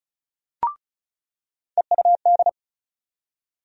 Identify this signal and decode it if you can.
Morse code, heard as a keyed tone, about 34 wpm, 700 Hz. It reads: EUD
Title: USA Call Sign Postfixes - 3 Letters 35wpm